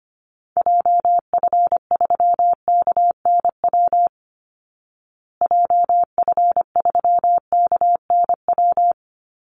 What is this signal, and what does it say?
Morse code, heard as a keyed tone, about 25 wpm, 700 Hz. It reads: JF3KNW JF3KNW
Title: Call Signs Repeated 3x - 25wpm